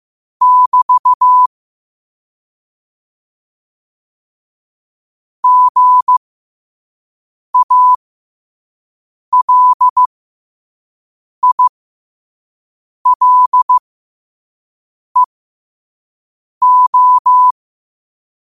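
Morse code, heard as a keyed tone, about 15 wpm, spaced as for 5 wpm, 1000 Hz. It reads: <BT> GALILEO
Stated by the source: Morse code quote-of-the-day practice at 5 wpm.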